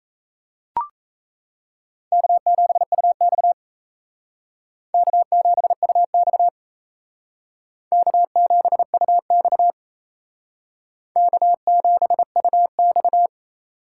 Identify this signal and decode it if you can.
Morse code, heard as a keyed tone, about 36 wpm, 700 Hz. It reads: K7UX K7UX K7UX K7UX